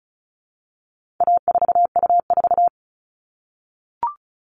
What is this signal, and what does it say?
Morse code, heard as a keyed tone, about 34 wpm, 700 Hz. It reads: A4U4
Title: All Cut Numbers - 4-Digit 35wpm